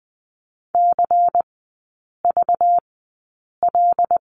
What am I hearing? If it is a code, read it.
Morse code, heard as a keyed tone, about 20 wpm, spaced as for 8 wpm, 700 Hz. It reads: CVL
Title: 191 Lesson 31e - Letter v - Three Characters 20wpm